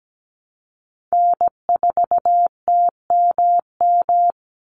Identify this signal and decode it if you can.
Morse code, heard as a keyed tone, about 17 wpm, 700 Hz. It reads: N4TMM
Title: Call Signs Repeated 3x - 17wpm